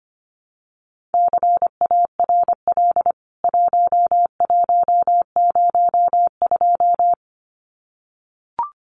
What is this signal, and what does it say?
Morse code, heard as a keyed tone, about 25 wpm, 700 Hz. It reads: CARL 1102
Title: CWT Exchange 25wpm